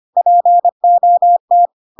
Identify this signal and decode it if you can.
Morse code, heard as a keyed tone, about 25 wpm, 700 Hz. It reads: POT
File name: Pinocchio / Pinocchio ch5 25 25 ik1hge